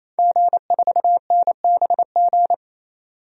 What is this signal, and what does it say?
Morse code, heard as a keyed tone, about 28 wpm, 700 Hz. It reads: G4NBG
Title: DXCC-100 Call Signs - 5 Characters - 28wpm